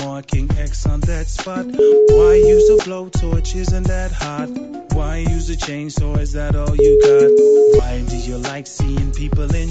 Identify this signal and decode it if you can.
Ringback tone (Europe, 425 Hz).